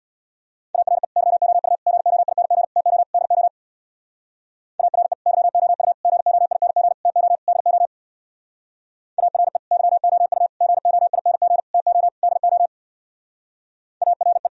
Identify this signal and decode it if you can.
Morse code, heard as a keyed tone, about 62 wpm, 700 Hz. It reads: ARE YOU GOING TO DO ARE YOU GOING TO DO ARE YOU GOING TO DO ARE
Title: Speed-Racing - Sets of 5 Words 50wpm